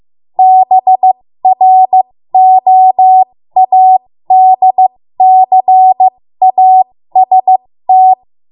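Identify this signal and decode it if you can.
Morse code, heard as a keyed tone, about 15 wpm, spaced as for 13 wpm, 750 Hz. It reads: BROADCAST
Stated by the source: 13 wpm Morse code practice archive transmission.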